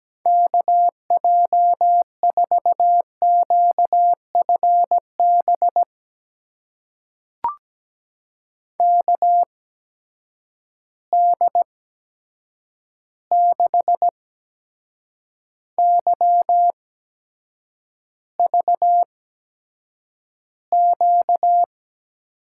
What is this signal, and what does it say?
Morse code, heard as a keyed tone, about 17 wpm, spaced as for 7 wpm, 700 Hz. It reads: KJ4QFB KD6YVQ
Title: USA Call Signs - 6 Characters - 8x Character Spacing - 17wpm